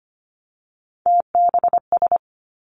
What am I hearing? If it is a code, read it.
Morse code, heard as a keyed tone, about 25 wpm, 700 Hz. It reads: TBS
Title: Three Challenging Letter-Number Combinations 25wpm